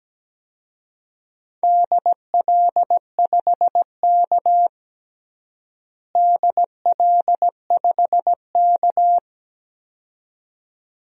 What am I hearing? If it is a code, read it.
Morse code, heard as a keyed tone, about 17 wpm, 700 Hz. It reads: DL5K DL5K
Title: DXCC-100 Call Signs - 4 Characters - Repeated 2x - 17wpm